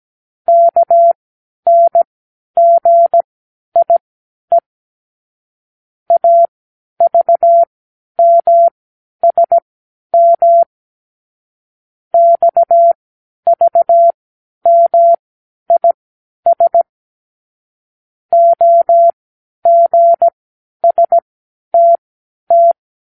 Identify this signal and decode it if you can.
Morse code, heard as a keyed tone, about 17 wpm, spaced as for 10 wpm, 680 Hz. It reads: KNGIE AVMSM XVMIS OGSTT
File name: k7qo / Lesson 36 (A-X)